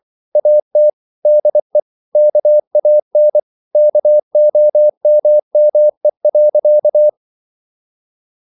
Morse code, heard as a keyed tone, about 24 wpm, 600 Hz. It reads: AT DE KAN KOMME.
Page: Ferie 23 wpm